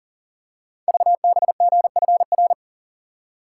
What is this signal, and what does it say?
Morse code, heard as a keyed tone, about 39 wpm, 700 Hz. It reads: VBGFR
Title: Code Groups - 5 Letters - 40wpm